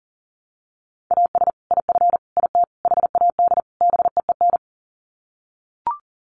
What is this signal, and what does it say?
Morse code, heard as a keyed tone, about 40 wpm, 700 Hz. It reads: AS IF IT HAD BEEN
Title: Sets of 5 Words Encore2 40wpm